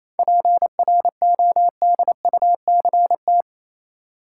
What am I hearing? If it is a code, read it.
Morse code, heard as a keyed tone, about 28 wpm, 700 Hz. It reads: PRODUCT
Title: Sentences from Top 1000 Words - Review Part 2 - New Word Drill - 28wpm